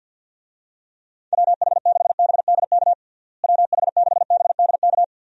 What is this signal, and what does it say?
Morse code, heard as a keyed tone, about 52 wpm, 700 Hz. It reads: WH6BDK WH6BDK
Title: USA Call Signs - 6 Characters - Repeated 2x - 50wpm